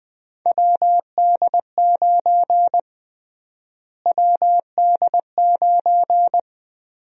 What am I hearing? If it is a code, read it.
Morse code, heard as a keyed tone, about 20 wpm, 700 Hz. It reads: WD9 WD9
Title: USA Call Sign Prefixes - Repeated 2x - 20wpm